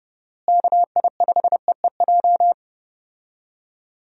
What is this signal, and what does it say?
Morse code, heard as a keyed tone, about 30 wpm, 700 Hz. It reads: KI5EEJ
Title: USA Call Signs - 6 Characters - Repeated 3x - 30wpm